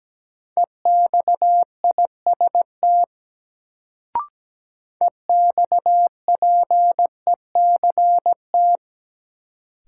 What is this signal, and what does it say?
Morse code, heard as a keyed tone, about 17 wpm, 700 Hz. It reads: EXIST EXPECT
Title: Sentences from Top 1100 Words - New Word Drill Alt - 17wpm